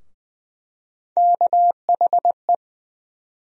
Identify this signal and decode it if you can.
Morse code, heard as a keyed tone, about 20 wpm, 700 Hz. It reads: KHE
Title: USA Call Sign Postfixes - 3 Letters 20wpm